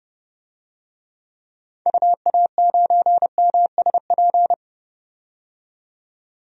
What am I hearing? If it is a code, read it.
Morse code, heard as a keyed tone, about 31 wpm, 700 Hz. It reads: UA9MSP
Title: DXCC-100 Call Signs - 6 Characters - 30wpm